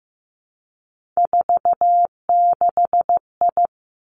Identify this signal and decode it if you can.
Morse code, heard as a keyed tone, about 15 wpm, 700 Hz. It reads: 46I
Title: Three Challenging Letter-Number Combinations 15wpm